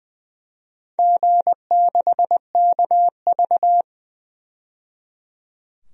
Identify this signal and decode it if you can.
Morse code, heard as a keyed tone, about 20 wpm, 700 Hz. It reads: G6KV
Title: DXCC-100 Call Signs - 4 Characters - 20wpm